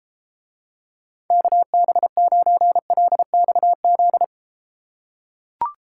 Morse code, heard as a keyed tone, about 32 wpm, 700 Hz. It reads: KB9LXZ